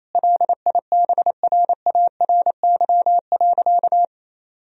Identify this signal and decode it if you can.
Morse code, heard as a keyed tone, about 28 wpm, 700 Hz. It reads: LIBRARY.
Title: Sentences from Top 1000 Words - Review Part 4 - 28wpm